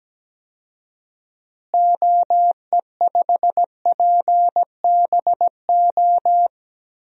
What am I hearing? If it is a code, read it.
Morse code, heard as a keyed tone, about 17 wpm, 700 Hz. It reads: OE5PBO
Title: DXCC-100 Call Signs - 6 Characters - Repeated 2x - 17wpm